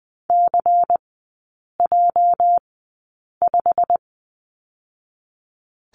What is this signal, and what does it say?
Morse code, heard as a keyed tone, about 19 wpm, spaced as for 8 wpm, 690 Hz. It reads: CJ5
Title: 205 Lesson 33e - Letter j - Three Characters 20wpm